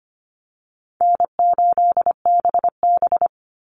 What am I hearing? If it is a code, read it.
Morse code, heard as a keyed tone, about 25 wpm, 700 Hz. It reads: N8BB
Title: NA SOTA Chaser Callsigns 25wpm